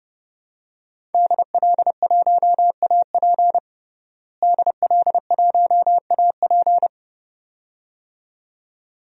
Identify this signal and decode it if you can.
Morse code, heard as a keyed tone, about 31 wpm, 700 Hz. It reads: DL1AP DL1AP
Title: DXCC-100 Call Signs - 5 Characters - Repeated 2x - 30wpm